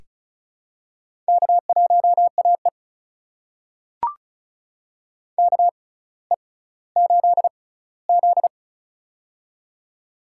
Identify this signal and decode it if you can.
Morse code, heard as a keyed tone, about 34 wpm, spaced as for 18 wpm, 700 Hz. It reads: K1AE KE8Z
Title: USA Call Signs - 4 Characters - 6x Character Spacing - 35wpm